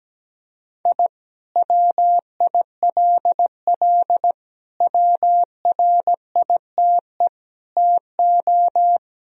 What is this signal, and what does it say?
Morse code, heard as a keyed tone, about 17 wpm, 700 Hz. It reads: I WILL WRITE TO
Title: Sentences from Top 1000 Words - Review Part 1 - 17wpm